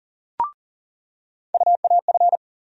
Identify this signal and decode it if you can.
Morse code, heard as a keyed tone, about 40 wpm, 700 Hz. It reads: UAF
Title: USA Call Sign Postfixes - 3 Letters 40wpm